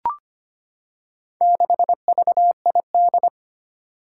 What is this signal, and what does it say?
Morse code, heard as a keyed tone, about 25 wpm, 700 Hz. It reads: 6VID
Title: Four Challenging Letter-Number Combinations 25wpm